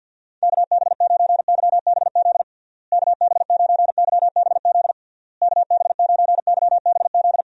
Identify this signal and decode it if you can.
Morse code, heard as a keyed tone, about 46 wpm, 700 Hz. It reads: KB9YBZ KB9YBZ KB9YBZ
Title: USA Call Signs - 6 Characters - Repeated 3x - 50wpm